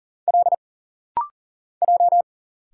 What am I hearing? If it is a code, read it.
Morse code, heard as a keyed tone, about 40 wpm, 700 Hz. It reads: L J
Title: Single Letters - Rapid-Fire 40wpm